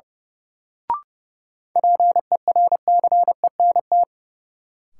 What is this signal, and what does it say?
Morse code, heard as a keyed tone, about 30 wpm, 700 Hz. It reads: PERCENT
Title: Sets of 2 Words - New Word Drill - 30wpm